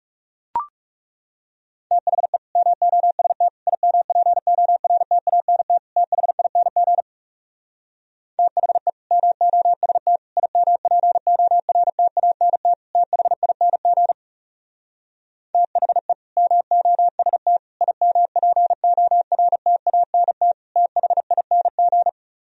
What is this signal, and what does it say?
Morse code, heard as a keyed tone, about 40 wpm, 700 Hz. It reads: THE MOST IMPORTANT THING THE MOST IMPORTANT THING THE MOST IMPORTANT THING